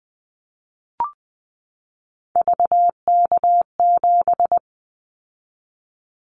Morse code, heard as a keyed tone, about 19 wpm, 700 Hz. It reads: VK7
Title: Callsign Prefixes for Top 100 DXCC - 20wpm